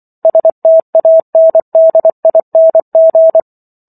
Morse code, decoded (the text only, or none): STANDING